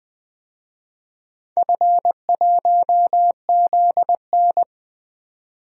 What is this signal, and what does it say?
Morse code, heard as a keyed tone, about 21 wpm, 700 Hz. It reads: F1ZN
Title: DXCC-100 Call Signs - 4 Characters - 20wpm